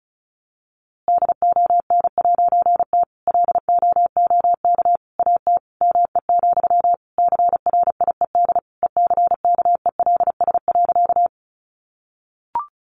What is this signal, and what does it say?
Morse code, heard as a keyed tone, about 34 wpm, 700 Hz. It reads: DON'T LOOK AT ME, CRIED ECKELS.
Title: A Sound Of Thunder 35wpm